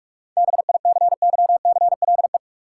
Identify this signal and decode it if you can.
Morse code, heard as a keyed tone, about 42 wpm, 700 Hz. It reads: BICYCLE
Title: Seven Letter Words - 45wpm